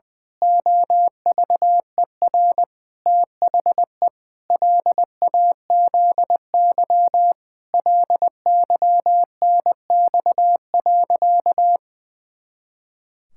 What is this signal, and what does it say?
Morse code, heard as a keyed tone, about 21 wpm, 700 Hz. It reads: OVER THE LAZY LYNX.
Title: Pangram Challenge v2 - 20wpm